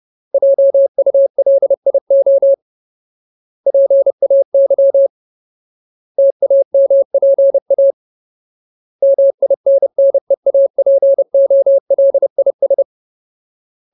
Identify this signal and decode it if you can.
Morse code, heard as a keyed tone, about 30 wpm, 550 Hz. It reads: JULIO PAY TAMPA MINNEAPOLIS
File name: Ultimate Word List / 30WPM Ultimate